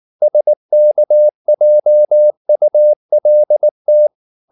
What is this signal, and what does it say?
Morse code, heard as a keyed tone, about 19 wpm, 600 Hz. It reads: SKJULT
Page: Ferie 19 wpm